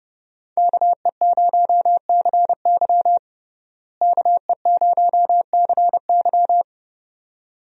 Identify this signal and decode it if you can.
Morse code, heard as a keyed tone, about 30 wpm, 700 Hz. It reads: KE0CY KE0CY
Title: USA Call Signs - 5 Characters - Repeated 2x - 30wpm